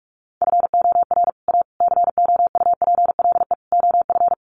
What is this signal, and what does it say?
Morse code, heard as a keyed tone, about 44 wpm, 700 Hz. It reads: FOR A COUPLE OF
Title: Sets of 4 Words 45wpm